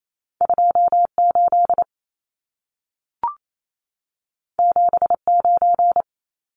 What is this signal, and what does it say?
Morse code, heard as a keyed tone, about 28 wpm, 700 Hz. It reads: 28 79
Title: Numbers - 2-Digit 28wpm